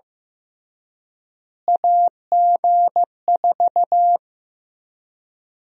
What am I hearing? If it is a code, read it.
Morse code, heard as a keyed tone, about 15 wpm, 700 Hz. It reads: AG4
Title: USA Call Sign Prefixes - Repeated 2x - 15wpm